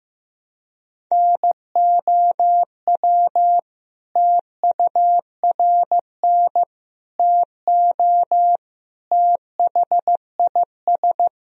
Morse code, heard as a keyed tone, about 15 wpm, 700 Hz. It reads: NOW TURN TO THIS
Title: Sentences from Top 1000 Words - Review Part 1 - 15wpm